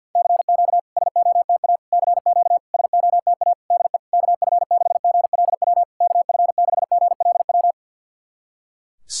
Morse code, heard as a keyed tone, about 52 wpm, 690 Hz. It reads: CQ SOTA CQ SOTA DE KF6GLW KF6GLW
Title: SOTA QSO Practice 50wpm